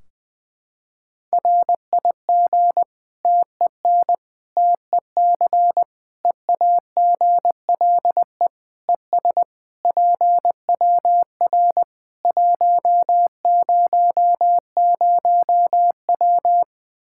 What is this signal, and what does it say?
Morse code, heard as a keyed tone, about 20 wpm, 700 Hz. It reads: RIG TEN TEC EAGLE ES PWR 100W
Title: Practice QSO - Exchange 2 - Info - 20wpm